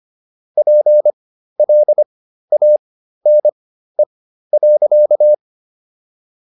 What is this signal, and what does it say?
Morse code, heard as a keyed tone, about 25 wpm, spaced as for 13 wpm, 600 Hz. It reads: PLANE.